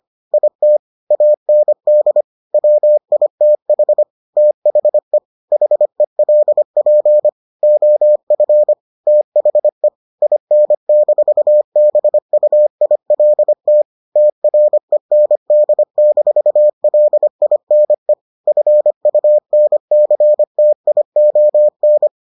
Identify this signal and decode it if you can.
Morse code, heard as a keyed tone, about 25 wpm, 600 Hz. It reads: IT AND WITH THE HELP OF THE IN-BUILT TREND-LINE FUNCTION